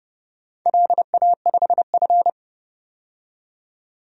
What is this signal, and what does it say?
Morse code, heard as a keyed tone, about 30 wpm, 700 Hz. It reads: LA5F
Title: DXCC-100 Call Signs - 4 Characters - 30wpm